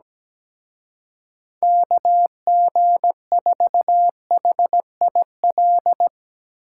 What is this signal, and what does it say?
Morse code, heard as a keyed tone, about 18 wpm, 700 Hz. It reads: KG4HIL
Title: Speed Racing - USA Call Signs 17wpm